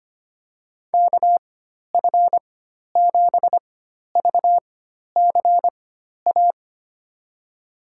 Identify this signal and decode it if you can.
Morse code, heard as a keyed tone, about 25 wpm, spaced as for 12 wpm, 700 Hz. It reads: KF7VCA
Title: USA Call Signs - 6 Characters - 4x Character Spacing - 25wpm